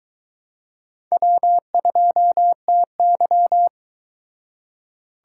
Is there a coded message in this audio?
Yes: Morse code from 1.1 to 3.7 s, heard as a keyed tone, about 23 wpm, 700 Hz. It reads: W2TY